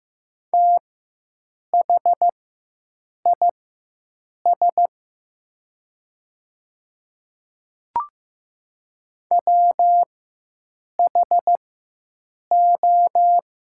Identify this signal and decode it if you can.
Morse code, heard as a keyed tone, about 15 wpm, spaced as for 7 wpm, 700 Hz. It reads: THIS WHO
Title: 4x Character Spacing - Top 300 Words - 15wpm